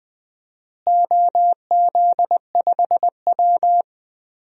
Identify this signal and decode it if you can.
Morse code, heard as a keyed tone, about 20 wpm, 700 Hz. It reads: OZ5W
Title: DXCC-100 Call Signs - 4 Characters - Repeated 2x - 20wpm